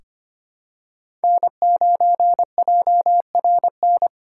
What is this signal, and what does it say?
Morse code, heard as a keyed tone, about 25 wpm, 700 Hz. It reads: N9JRN
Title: USA Call Signs - 5 Characters - 3x Character Spacing - 25wpm